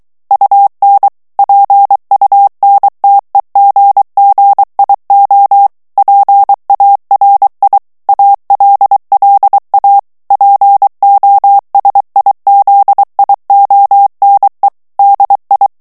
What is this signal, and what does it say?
Morse code, heard as a keyed tone, about 23 wpm, 800 Hz. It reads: UN PUNTEGGIO PARI ALLA POSIZIONE DI
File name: ITALIEN texte clair 2 vitesse 22 mots minute